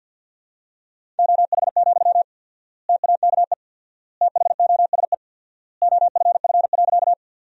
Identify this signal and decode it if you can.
Morse code, heard as a keyed tone, about 47 wpm, 700 Hz. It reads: OH, TAKE THOSE OFF.
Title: 4x Word-Spacing - Sentences from Top 500 Words - 50wpm